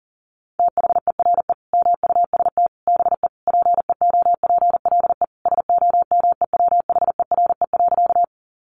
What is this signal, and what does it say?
Morse code, heard as a keyed tone, about 40 wpm, 700 Hz. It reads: THERE MUST BE PEOPLE SOMEWHERE.
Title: Sentences from Top 1400 Words 40wpm